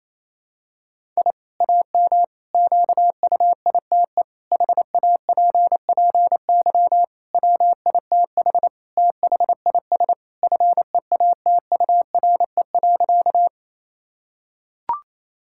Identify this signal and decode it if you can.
Morse code, heard as a keyed tone, about 28 wpm, 700 Hz. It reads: I AM QUITE HAPPY WITH THIS FEATURE.
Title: Sentences from Top 1000 Words - Review Part 5 - 28wpm